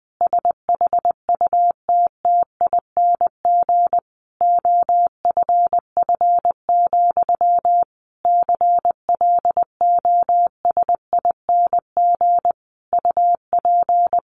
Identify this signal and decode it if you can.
Morse code, heard as a keyed tone, about 20 wpm, 700 Hz. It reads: SHUTTING OFF, CLOSING UP
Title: A Sound Of Thunder 20wpm